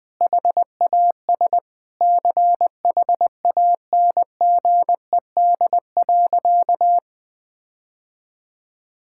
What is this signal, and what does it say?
Morse code, heard as a keyed tone, about 20 wpm, 700 Hz. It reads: HAS CHANGED.
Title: Sentences with 5-Word Phrases - 20wpm